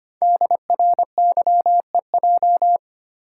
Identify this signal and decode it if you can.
Morse code, heard as a keyed tone, about 25 wpm, 700 Hz. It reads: DRYEJ
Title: Code Groups - 5 Letters - 25wpm